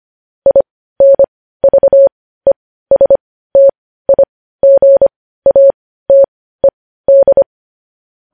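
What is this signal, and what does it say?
Morse code, heard as a keyed tone, about 25 wpm, spaced as for 15 wpm, 550 Hz. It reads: INVESTIGATED
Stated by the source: News headlines sent in Morse code at 15 wpm.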